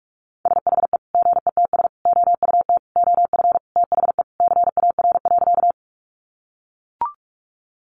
Morse code, heard as a keyed tone, about 44 wpm, 700 Hz. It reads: SHE GETS OUT OF THE CAR.
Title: Easy Phrases 45wpm